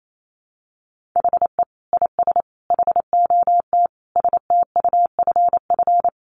Morse code, heard as a keyed tone, about 28 wpm, 700 Hz. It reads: HE IS HOT STUFF
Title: Sentences from Top 1500 Words 28wpm